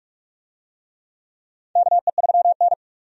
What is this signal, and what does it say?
Morse code, heard as a keyed tone, about 45 wpm, 700 Hz. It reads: KE3N
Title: USA Call Signs - 4 Characters - 45wpm